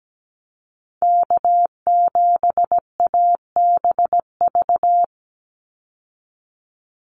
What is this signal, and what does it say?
Morse code, heard as a keyed tone, about 17 wpm, 700 Hz. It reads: K7ABV
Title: Call Signs Repeated 3x - 17wpm